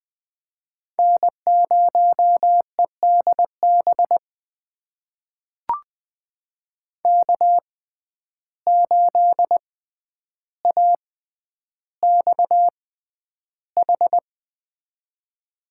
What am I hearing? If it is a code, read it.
Morse code, heard as a keyed tone, about 20 wpm, spaced as for 10 wpm, 700 Hz. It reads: N0EDB K8AXH
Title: USA Call Signs - 5 Characters - 6x Character Spacing - 20wpm